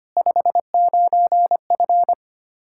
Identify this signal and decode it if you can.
Morse code, heard as a keyed tone, about 25 wpm, 700 Hz. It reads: 59F
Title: Practice QSO - Exchange 2 - Info - 25wpm